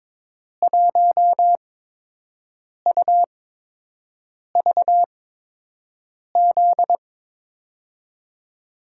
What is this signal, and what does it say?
Morse code, heard as a keyed tone, about 22 wpm, 700 Hz. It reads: 1 U V Z
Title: USA Call Signs - 5 Characters - 8x Character Spacing - 22wpm